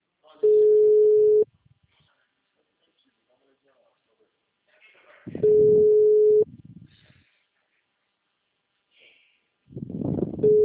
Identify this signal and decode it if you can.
Ringback tone (Europe, 425 Hz), with speech.